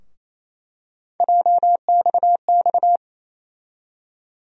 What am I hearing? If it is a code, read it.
Morse code, heard as a keyed tone, about 28 wpm, 700 Hz. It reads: JXX
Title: USA Call Sign Postfixes - 3 Letters 28wpm